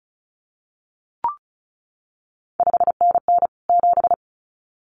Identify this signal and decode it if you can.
Morse code, heard as a keyed tone, about 36 wpm, 700 Hz. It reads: 5NN 7
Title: CQ WW DX Contest Exchange - 35wpm